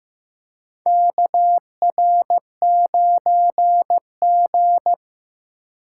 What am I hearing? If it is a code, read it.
Morse code, heard as a keyed tone, about 15 wpm, 700 Hz. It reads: KR9G
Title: USA Call Signs - 3x Character Spacing - 15wpm